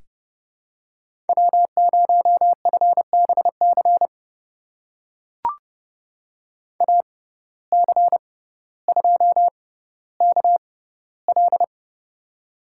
Morse code, heard as a keyed tone, about 30 wpm, spaced as for 15 wpm, 700 Hz. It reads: W0FBC AC2KL
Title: USA Call Signs - 5 Characters - 6x Character Spacing - 30wpm